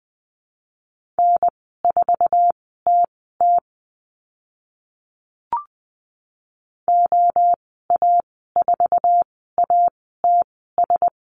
Morse code, heard as a keyed tone, about 20 wpm, spaced as for 15 wpm, 700 Hz. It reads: N4TT OA4ATS
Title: 082 Lesson 14g - Number 2 - Call Signs 20wpm